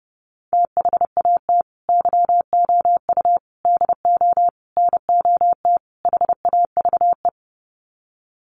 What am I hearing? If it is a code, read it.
Morse code, heard as a keyed tone, about 30 wpm, 700 Hz. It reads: THAT YOU DO NOT HAVE
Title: Sets of 5 Words Encore 30wpm